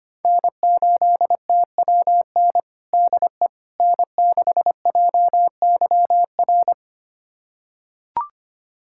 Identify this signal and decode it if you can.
Morse code, heard as a keyed tone, about 25 wpm, 700 Hz. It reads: N8TWN DE N6JYR
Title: USA Callsign DE Callsign 25wpm